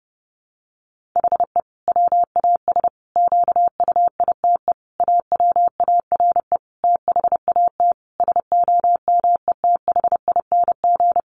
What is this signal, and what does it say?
Morse code, heard as a keyed tone, about 30 wpm, 700 Hz. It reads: HE WAS QUITE AWARE THAT SOMETHING
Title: Sentences from Top 1400 Words 30wpm